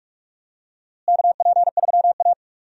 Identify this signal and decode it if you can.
Morse code, heard as a keyed tone, about 47 wpm, 700 Hz. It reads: KP3A
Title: USA Call Signs - 4 Characters - Repeated 2x - 45wpm